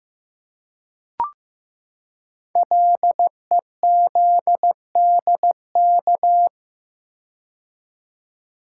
Morse code, heard as a keyed tone, about 15 wpm, 700 Hz. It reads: LEZDK
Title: Code Groups - 5 Letters - 15wpm